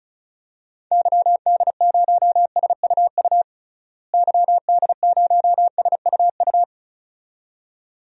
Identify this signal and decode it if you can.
Morse code, heard as a keyed tone, about 36 wpm, 700 Hz. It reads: YD0SUU YD0SUU
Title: DXCC-100 Call Signs - 6 Characters - Repeated 2x - 35wpm